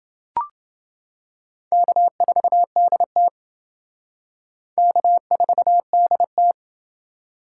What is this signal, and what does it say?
Morse code, heard as a keyed tone, about 29 wpm, 700 Hz. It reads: K4DT K4DT